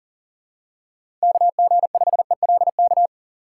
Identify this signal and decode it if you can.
Morse code, heard as a keyed tone, about 37 wpm, 700 Hz. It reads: KG5ELK
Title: USA Call Signs - 6 Characters - 6x Character Spacing - 40wpm